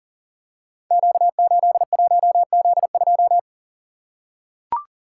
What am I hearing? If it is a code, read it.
Morse code, heard as a keyed tone, about 38 wpm, 700 Hz. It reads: Q81Z2
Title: Code Groups - 5 Letter:Numbers - 40wpm